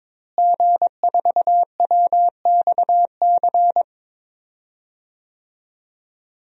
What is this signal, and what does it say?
Morse code, heard as a keyed tone, about 22 wpm, 700 Hz. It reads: G4WXC
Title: DXCC-100 Call Signs - 5 Characters - Repeated 2x - 22wpm